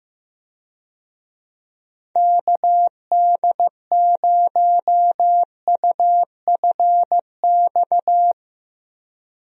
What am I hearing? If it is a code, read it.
Morse code, heard as a keyed tone, about 16 wpm, 700 Hz. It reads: KD0UFX
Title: USA Call Signs - 6 Characters - Repeated 2x - 15wpm